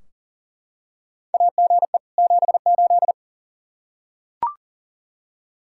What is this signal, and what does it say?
Morse code, heard as a keyed tone, about 39 wpm, 700 Hz. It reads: AGE 78
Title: Practice QSO - Exchange 3 - More Info - 40wpm